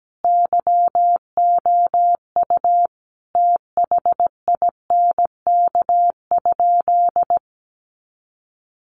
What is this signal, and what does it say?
Morse code, heard as a keyed tone, about 17 wpm, 700 Hz. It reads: YOU THINK?
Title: Sentences from Top 200 Words 17wpm